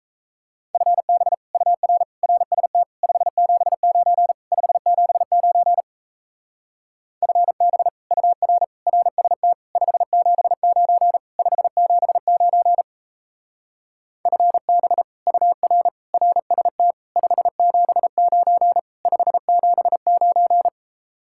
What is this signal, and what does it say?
Morse code, heard as a keyed tone, about 38 wpm, 700 Hz. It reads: FB UR RST 579 579 FB UR RST 579 579 FB UR RST 579 579